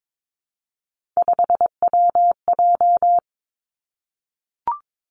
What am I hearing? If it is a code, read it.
Morse code, heard as a keyed tone, about 22 wpm, 700 Hz. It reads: 5WJ
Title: Three Challenging Letter-Number Combinations 22wpm